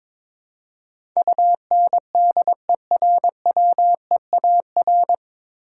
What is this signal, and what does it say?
Morse code, heard as a keyed tone, about 22 wpm, 700 Hz. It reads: UNDERWEAR
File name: 9 Letter Words - First 3 Letters Given - 22wpm